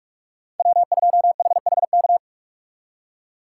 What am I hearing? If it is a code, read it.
Morse code, heard as a keyed tone, about 54 wpm, 700 Hz. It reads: W2HHK
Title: Speed Racing - USA 1x3 Call Signs 45wpm